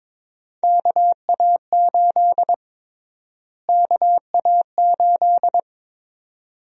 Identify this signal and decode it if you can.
Morse code, heard as a keyed tone, about 22 wpm, 700 Hz. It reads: KA8 KA8
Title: USA Call Sign Prefixes - Repeated 2x - 22wpm